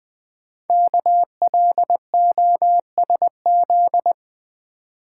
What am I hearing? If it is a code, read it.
Morse code, heard as a keyed tone, about 20 wpm, 700 Hz. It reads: KLOSZ
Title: Code Groups - 5 Letters - 20wpm